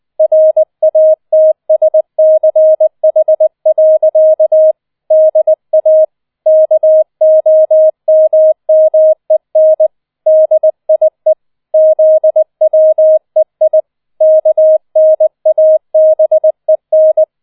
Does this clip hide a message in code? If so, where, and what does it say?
Yes: Morse code from 0.2 to 17.2 s, heard as a keyed tone, about 20 wpm, 630 Hz. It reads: RATSCH. DA KOMMEN DIE ZWEI KNABEN